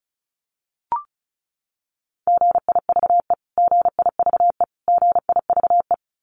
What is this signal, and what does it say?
Morse code, heard as a keyed tone, about 34 wpm, 700 Hz. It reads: GIVE GIVE GIVE
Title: Top 500 Words Repeated 35wpm